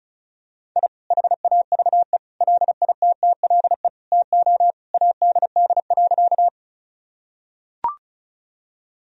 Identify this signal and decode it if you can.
Morse code, heard as a keyed tone, about 34 wpm, 700 Hz. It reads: I HAVE LITTLE TO ADD.
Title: Sentences from Top 1000 Words - Review Part 1 - 35wpm